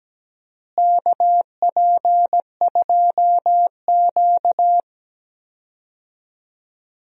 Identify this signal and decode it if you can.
Morse code, heard as a keyed tone, about 17 wpm, 700 Hz. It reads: KP2Q
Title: DXCC-100 Call Signs - 4 Characters - Repeated 2x - 17wpm